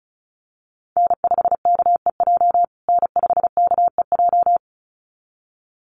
Morse code, heard as a keyed tone, about 34 wpm, 700 Hz. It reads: N5KEJ N5KEJ
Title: USA Call Signs Repeated 2x 35wpm